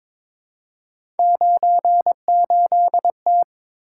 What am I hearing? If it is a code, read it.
Morse code, heard as a keyed tone, about 22 wpm, 700 Hz. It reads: 98T
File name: practice / 22wpm / Three Letter-Number Combinations - 22wpm